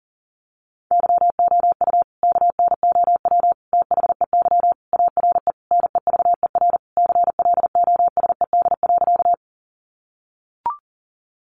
Morse code, heard as a keyed tone, about 40 wpm, 700 Hz. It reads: YOU KNOW THEY ARE NEVER CLOSED.
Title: Sentences from Top 1400 Words 40wpm